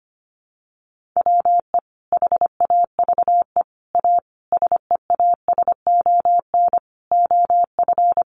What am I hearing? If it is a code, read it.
Morse code, heard as a keyed tone, about 25 wpm, 700 Hz. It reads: WE HAVE A SEASON OF
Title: Sentences from Top 1400 Words 25wpm